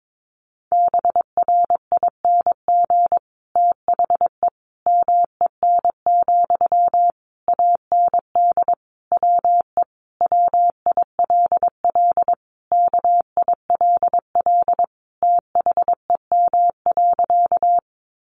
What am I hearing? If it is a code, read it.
Morse code, heard as a keyed tone, about 22 wpm, 700 Hz. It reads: BRING THE MEN, AND WE WILL KILL THEM.
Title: Sentences from Top 1400 Words 22wpm